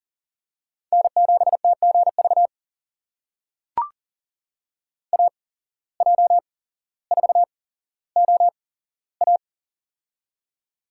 Morse code, heard as a keyed tone, about 39 wpm, spaced as for 17 wpm, 700 Hz. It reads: N7TGV AJ4OA
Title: USA Call Signs - 5 Characters - 8x Character Spacing - 40wpm